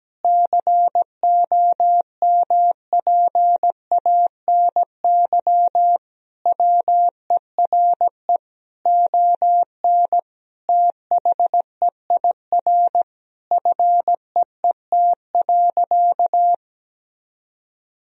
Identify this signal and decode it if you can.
Morse code, heard as a keyed tone, about 17 wpm, 700 Hz. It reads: COMPANY WERE ON THEIR FEET.
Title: Sentences from Top 1000 Words - Review Part 3 - 17wpm